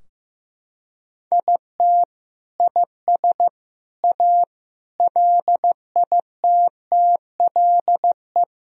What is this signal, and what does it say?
Morse code, heard as a keyed tone, about 15 wpm, 700 Hz. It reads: IT IS A LITTLE
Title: Sentences from Top 1000 Words - Review Part 2 - 15wpm